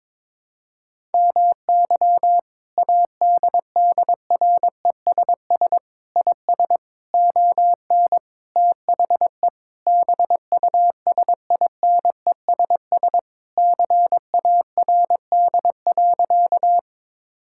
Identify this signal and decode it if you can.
Morse code, heard as a keyed tone, about 22 wpm, 700 Hz. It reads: MY ADDRESS IS ON THE BUSINESS CARD.
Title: Sentences from Top 1000 Words - Review Part 6 - 22wpm